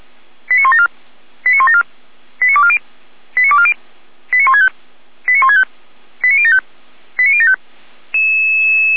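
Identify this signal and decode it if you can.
Five-tone selective call (ZVEI), heard as quick runs of five tones, tones of about 70 ms. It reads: 89174 89174 89240 89240 89165 89165 89075 89075
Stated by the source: ZVEI five-tone selective call, code 89337, received by SDR, from Feuerwehr Orlamünde.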